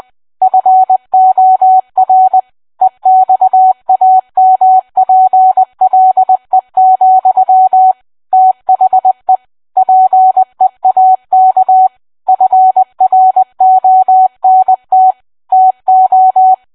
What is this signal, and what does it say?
Morse code, heard as a keyed tone, about 20 wpm, 750 Hz. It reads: FOR EXAMPLE, THE PEAK FRONT TO